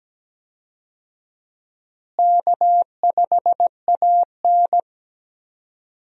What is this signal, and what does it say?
Morse code, heard as a keyed tone, about 17 wpm, 700 Hz. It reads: K5AN
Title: USA Call Signs - 4 Characters - Repeated 2x - 17wpm